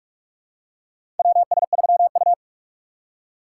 Morse code, heard as a keyed tone, about 42 wpm, 700 Hz. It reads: WS3U
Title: USA Call Signs - 4 Characters - 2x Character Spacing - 45wpm